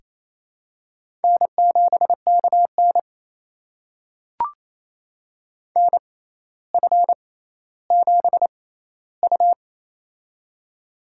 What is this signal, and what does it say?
Morse code, heard as a keyed tone, about 28 wpm, spaced as for 14 wpm, 700 Hz. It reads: N7KN NF7U
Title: USA Call Signs - 4 Characters - 6x Character Spacing - 28wpm